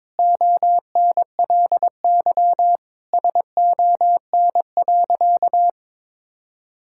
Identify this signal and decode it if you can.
Morse code, heard as a keyed tone, about 22 wpm, 700 Hz. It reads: ONLY SON.
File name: practice / 22wpm / Sentences with 8-Word Phrases - 22wpm